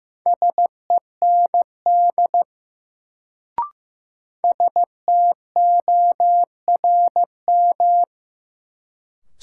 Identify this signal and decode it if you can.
Morse code, heard as a keyed tone, about 15 wpm, 700 Hz. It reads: SEND STORM
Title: Sentences from Top 1500 Words - New Word Drill Alt - 15wpm